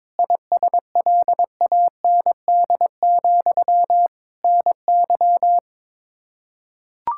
Morse code, heard as a keyed tone, about 22 wpm, 700 Hz. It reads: ISLAND, NY
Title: Practice QSO - Exchange 1 - Beginning - 22wpm